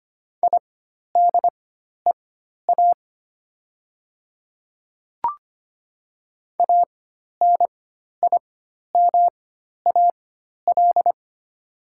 Morse code, heard as a keyed tone, about 25 wpm, spaced as for 12 wpm, 700 Hz. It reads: IDEA ANIMAL
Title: 4x Character Spacing - Top 300 Words - 25wpm